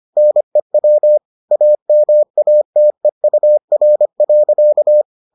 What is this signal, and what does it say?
Morse code, heard as a keyed tone, about 25 wpm, 600 Hz. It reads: NEW AMATEUR.